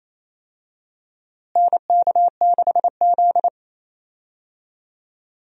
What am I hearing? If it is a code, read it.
Morse code, heard as a keyed tone, about 28 wpm, 700 Hz. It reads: NK6Z
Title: USA Call Signs - 4 Characters - Repeated 3x - 28wpm